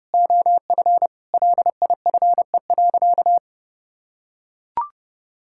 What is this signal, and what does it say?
Morse code, heard as a keyed tone, about 32 wpm, 700 Hz. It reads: OF LIFE.
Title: Sentences from Top 1000 Words - Review Part 6 - 30wpm